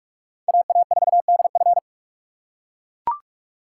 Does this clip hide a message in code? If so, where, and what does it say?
Yes: Morse code from 0.5 to 1.8 s, heard as a keyed tone, about 41 wpm, 700 Hz. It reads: AA4DF